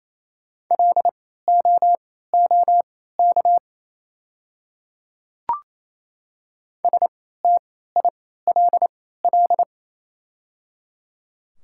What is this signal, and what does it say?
Morse code, heard as a keyed tone, about 28 wpm, spaced as for 16 wpm, 700 Hz. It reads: LOOK STILL
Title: 3x Character Spacing - Top 300 Words - 28wpm